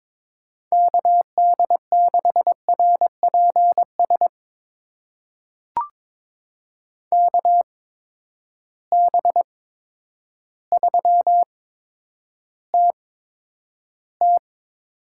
Morse code, heard as a keyed tone, about 21 wpm, spaced as for 10 wpm, 700 Hz. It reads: KD6RPS KB3TT
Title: USA Call Signs - 6 Characters - 8x Character Spacing - 22wpm